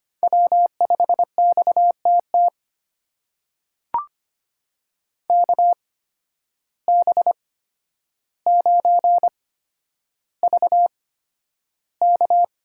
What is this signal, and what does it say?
Morse code, heard as a keyed tone, about 25 wpm, spaced as for 11 wpm, 700 Hz. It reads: W5XTT KB9VK
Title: USA Call Signs - 5 Characters - 8x Character Spacing - 25wpm